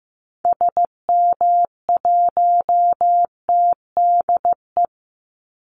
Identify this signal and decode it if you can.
Morse code, heard as a keyed tone, about 15 wpm, 700 Hz. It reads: SM1TDE
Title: Call Signs Repeated 2x - 15wpm